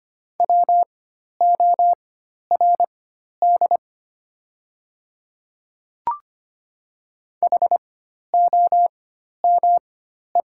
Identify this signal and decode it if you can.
Morse code, heard as a keyed tone, about 25 wpm, spaced as for 12 wpm, 700 Hz. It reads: WORD HOME
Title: 4x Character Spacing - Top 200 Words - 25wpm